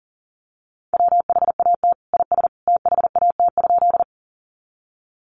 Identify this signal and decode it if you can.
Morse code, heard as a keyed tone, about 40 wpm, 700 Hz. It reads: WHAT IS THAT?
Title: Sentences from Top 100 Words 40wpm